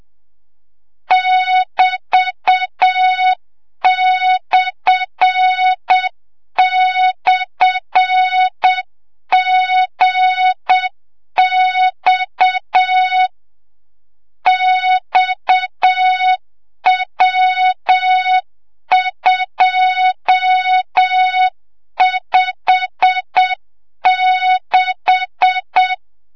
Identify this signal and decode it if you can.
Morse code, heard as a keyed tone, about 7 wpm, 3700 Hz, with noise. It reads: <BT>//GX XW256
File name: Morze-HA2MN / hang / 25 GYAKORLAT